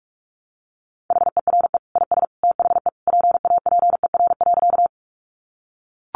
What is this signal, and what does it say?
Morse code, heard as a keyed tone, about 44 wpm, 700 Hz. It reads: HERE IS THE PAPER.
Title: Sentences from Top 300 Words 45wpm